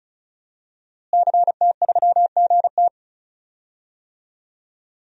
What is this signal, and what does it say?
Morse code, heard as a keyed tone, about 34 wpm, 700 Hz. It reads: CT3GT
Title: DXCC-100 Call Signs - 5 Characters - Repeated 2x - 35wpm